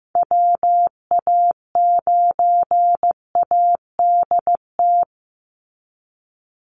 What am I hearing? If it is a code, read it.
Morse code, heard as a keyed tone, about 15 wpm, 700 Hz. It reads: WA9ADT
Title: Call Signs Repeated 3x - 15wpm